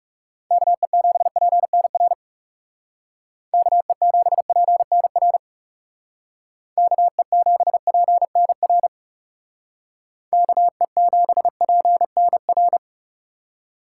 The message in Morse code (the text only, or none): KE7PNR KE7PNR KE7PNR KE7PNR